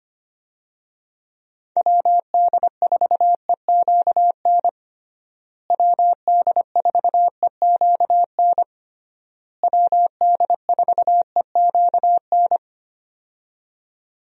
Morse code, heard as a keyed tone, about 25 wpm, 700 Hz. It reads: WD4EQN WD4EQN WD4EQN
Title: USA Call Signs - 6 Characters - Repeated 3x - 25wpm